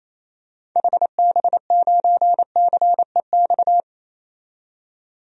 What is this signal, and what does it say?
Morse code, heard as a keyed tone, about 27 wpm, 700 Hz. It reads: HB9CEX
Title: DXCC-100 Call Signs - 6 Characters - 28wpm